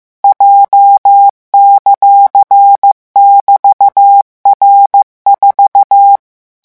Morse code, heard as a keyed tone, about 15 wpm, 800 Hz. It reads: J;<BT>R4
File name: morse mp3 / revision14-vitesse 14 mots